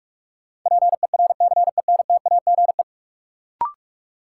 Morse code, heard as a keyed tone, about 45 wpm, 700 Hz. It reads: PERCENTAGE
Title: Sentences from Top 1300 Words - New Word Drill Alt - 45wpm